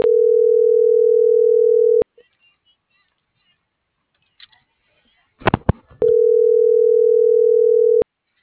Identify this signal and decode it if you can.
Ringback tone (North America), 440 + 480 Hz.